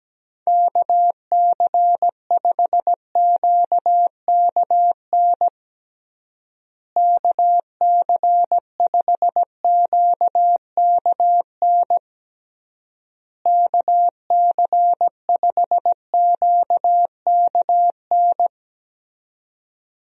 Morse code, heard as a keyed tone, about 17 wpm, 700 Hz. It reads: KC5QKN KC5QKN KC5QKN
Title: USA Call Signs - 6 Characters - Repeated 3x - 17wpm